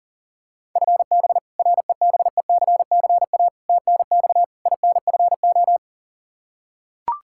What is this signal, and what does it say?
Morse code, heard as a keyed tone, about 37 wpm, 700 Hz. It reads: FB REBECCA TNX INFO
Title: Practice QSO - Exchange 3 - More Info - 40wpm